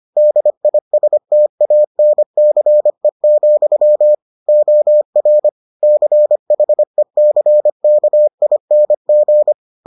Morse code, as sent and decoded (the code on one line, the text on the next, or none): -.. .. ... - .- -. -.-. --..-- / --- .-. / -.-. .... -.-. -.- .. -. --.
DISTANCE, OR CHECKING